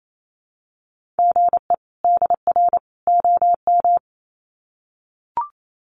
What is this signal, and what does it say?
Morse code, heard as a keyed tone, about 28 wpm, 700 Hz. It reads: GE DR OM
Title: Common QSO Phrases 28wpm